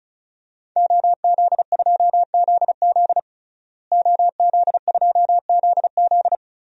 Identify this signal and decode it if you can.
Morse code, heard as a keyed tone, about 34 wpm, 700 Hz. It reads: OZ2ZZ OZ2ZZ
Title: DXCC-100 Call Signs - 5 Characters - Repeated 2x - 35wpm